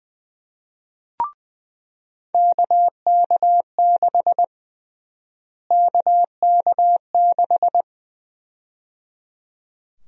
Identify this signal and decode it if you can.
Morse code, heard as a keyed tone, about 20 wpm, 700 Hz. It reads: KK6 KK6
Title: USA Call Sign Prefixes - Repeated 2x - 20wpm